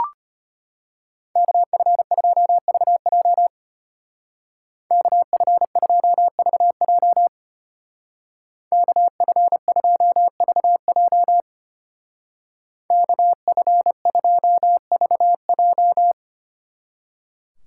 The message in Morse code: KF2VJ KF2VJ KF2VJ KF2VJ

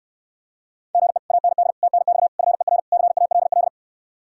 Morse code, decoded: GE AND TNX FER CALL

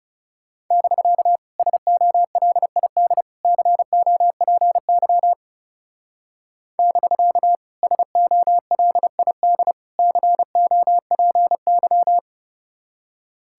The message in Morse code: <BK> SOLID COPY <BK> SOLID COPY

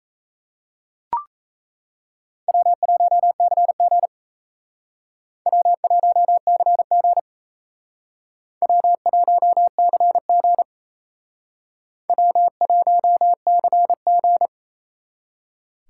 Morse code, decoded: W1CG W1CG W1CG W1CG

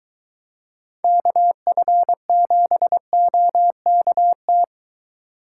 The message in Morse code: KF7OKT

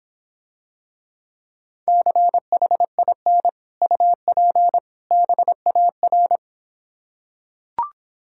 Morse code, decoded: CHIN UP BAR

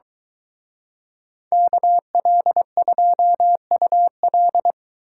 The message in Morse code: KL2UL

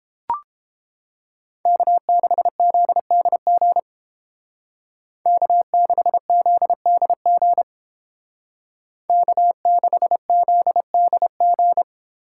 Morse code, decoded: K6ZDG K6ZDG K6ZDG

